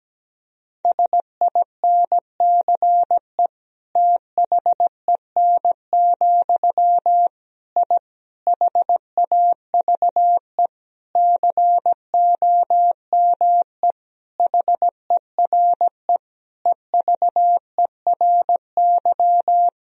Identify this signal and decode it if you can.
Morse code, heard as a keyed tone, about 17 wpm, 700 Hz. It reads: SINCE THEN, I HAVE COME HERE EVERY
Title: Sentences from Top 1000 Words - Review Part 1 - 17wpm